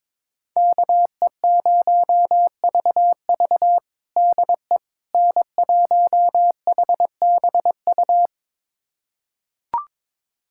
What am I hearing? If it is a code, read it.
Morse code, heard as a keyed tone, about 22 wpm, 700 Hz. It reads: KE0VV DE N1HBU
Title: USA Callsign DE Callsign - 5 Characters - 22wpm